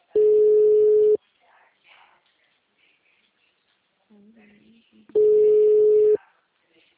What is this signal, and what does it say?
Ringback tone (Europe, 425 Hz), with speech.
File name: out- -2010- - - .18835